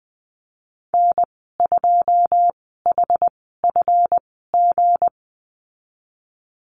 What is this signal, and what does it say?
Morse code, heard as a keyed tone, about 20 wpm, spaced as for 15 wpm, 700 Hz. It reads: N2HFG
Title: 179 Lesson 29g - Slash - Call Signs 20wpm